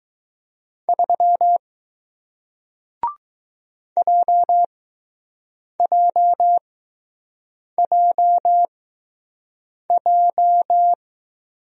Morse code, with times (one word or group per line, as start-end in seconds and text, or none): 0.9-1.6 3
4.0-10.9 JJJJ